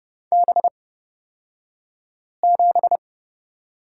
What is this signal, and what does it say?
Morse code, heard as a keyed tone, about 30 wpm, spaced as for 5 wpm, 700 Hz. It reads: B7
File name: Lesson 38d - Letter q - Two Characters 30wpm